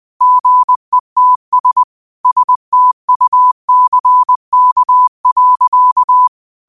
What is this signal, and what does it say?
Morse code, heard as a keyed tone, about 20 wpm, 1000 Hz. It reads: GETS STUCK.
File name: qotdfiles / 20wpm